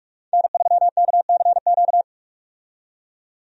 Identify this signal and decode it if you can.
Morse code, heard as a keyed tone, about 43 wpm, 700 Hz. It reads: N3KCQ
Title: USA Call Signs - 5 Characters - 45wpm